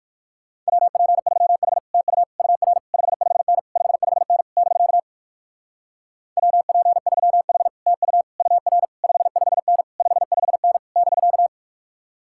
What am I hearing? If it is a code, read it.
Morse code, heard as a keyed tone, about 60 wpm, 700 Hz. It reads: WP3H TU UR 55N 55N <BK> WP3H TU UR 55N 55N <BK>